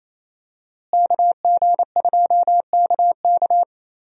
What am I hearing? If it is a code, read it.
Morse code, heard as a keyed tone, about 28 wpm, 700 Hz. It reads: KG2KK
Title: USA Call Signs - 5 Characters - Repeated 3x - 28wpm